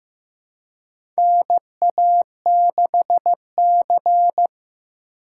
Morse code, heard as a keyed tone, about 15 wpm, 700 Hz. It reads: NA6C
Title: USA Call Signs - 4 Characters - 2x Character Spacing - 15wpm